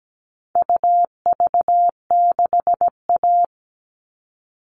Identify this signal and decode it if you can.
Morse code, heard as a keyed tone, about 17 wpm, 700 Hz. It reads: UV6A
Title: All Cut Numbers - 4-Digit 17wpm